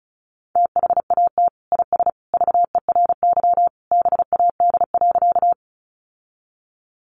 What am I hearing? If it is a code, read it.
Morse code, heard as a keyed tone, about 34 wpm, 700 Hz. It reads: THAT IS VERY BAD.
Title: Sentences from Top 700 Words 35wpm